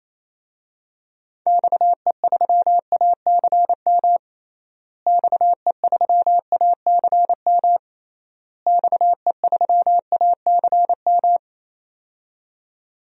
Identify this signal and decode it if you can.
Morse code, heard as a keyed tone, about 28 wpm, 700 Hz. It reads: XE3ACM XE3ACM XE3ACM
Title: DXCC-100 Call Signs - 6 Characters - Repeated 3x - 28wpm